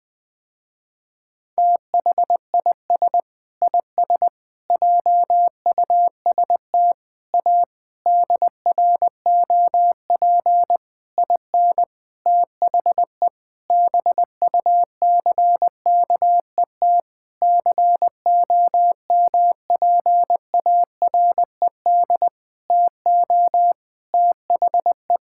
Morse code, heard as a keyed tone, about 21 wpm, 700 Hz. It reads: THIS IS JUST A DROP IN THE BUCKET COMPARED TO THE
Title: Sentences with 7-Word Phrases - Spoken Beginning - 20wpm